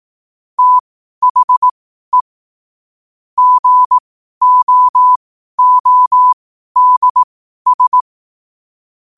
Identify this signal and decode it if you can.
Morse code, heard as a keyed tone, about 18 wpm, spaced as for 12 wpm, 1000 Hz. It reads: THE GOODS
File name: qotdfiles / 12wpm